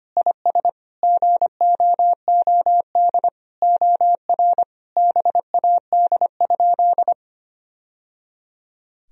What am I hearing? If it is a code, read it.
Morse code, heard as a keyed tone, about 25 wpm, 700 Hz. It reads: IS GOOD OR BAD?
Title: Sentences from Top 1000 Words - Review Part 1 - 25wpm